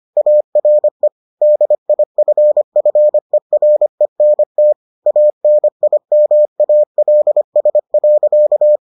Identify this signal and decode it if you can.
Morse code, heard as a keyed tone, about 25 wpm, 600 Hz. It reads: ARE DIFFERENT ANIMALS.